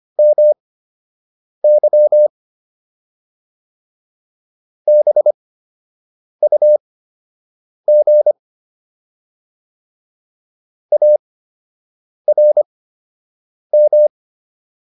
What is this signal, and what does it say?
Morse code, heard as a keyed tone, about 25 wpm, spaced as for 7 wpm, 600 Hz. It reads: MY BUG ARM